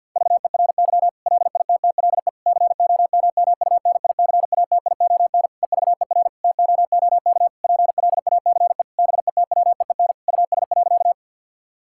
Morse code, heard as a keyed tone, about 50 wpm, 700 Hz. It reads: VERY LITTLE COMMUNICATION EVER TOOK PLACE BETWEEN US.
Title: Sentences from Top 1000 Words - Review Part 6 - 50wpm